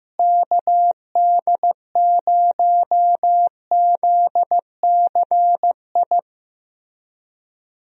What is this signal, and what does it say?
Morse code, heard as a keyed tone, about 15 wpm, 700 Hz. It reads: KD0ZCI
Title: USA Call Signs - 6 Characters - Repeated 3x - 15wpm